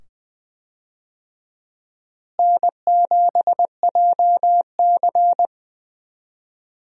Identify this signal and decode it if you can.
Morse code, heard as a keyed tone, about 20 wpm, 700 Hz. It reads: N7JC
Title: USA Call Signs - 4 Characters - Repeated 2x - 20wpm